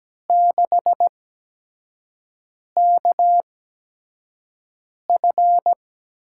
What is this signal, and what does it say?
Morse code, heard as a keyed tone, about 17 wpm, spaced as for 5 wpm, 700 Hz. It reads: 6KF
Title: USA Call Signs - 5 Characters - 8x Character Spacing - 17wpm